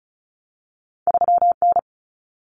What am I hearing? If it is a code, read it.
Morse code, heard as a keyed tone, about 36 wpm, 700 Hz. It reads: 3N
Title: Common Cut Numbers - 2-Digit 35wpm